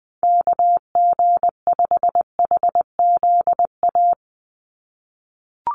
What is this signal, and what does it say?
Morse code, heard as a keyed tone, about 20 wpm, 700 Hz. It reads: KG5HZA
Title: USA Call Signs 20wpm